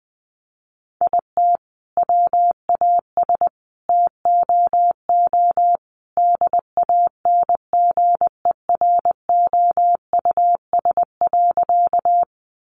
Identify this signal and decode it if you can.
Morse code, heard as a keyed tone, about 20 wpm, 700 Hz. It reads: IT WAS TOO DANGEROUS.
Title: Sentences from Top 1200 Words 20wpm